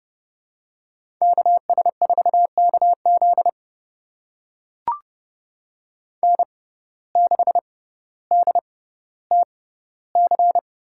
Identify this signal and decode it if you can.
Morse code, heard as a keyed tone, about 30 wpm, spaced as for 15 wpm, 700 Hz. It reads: KS4KZ N6DTC
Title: USA Call Signs - 6x Character Spacing - 30wpm